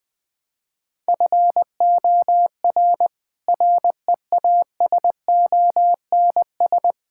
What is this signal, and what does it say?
Morse code, heard as a keyed tone, about 20 wpm, 700 Hz. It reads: FOR REASONS
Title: Sets of 8 Words - 20wpm